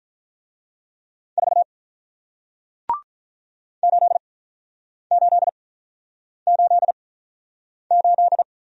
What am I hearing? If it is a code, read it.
Morse code, heard as a keyed tone, about 48 wpm, spaced as for 9 wpm, 690 Hz. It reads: 4 8888